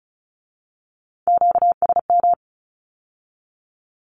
Morse code, heard as a keyed tone, about 36 wpm, 700 Hz. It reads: QSM
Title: Q-Codes with Spoken Meaning - 35wpm